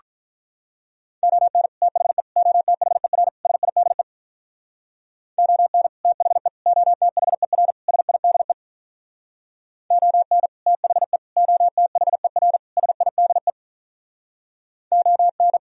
Morse code, heard as a keyed tone, about 45 wpm, 700 Hz. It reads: ON THE OTHER SIDE ON THE OTHER SIDE ON THE OTHER SIDE ON